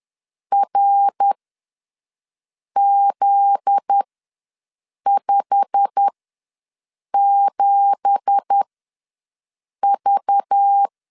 Morse code, heard as a keyed tone, about 10 wpm, spaced as for 5 wpm, 800 Hz. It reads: RZ57V
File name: LEZIONI / 9 2867 ws 10wpm 4215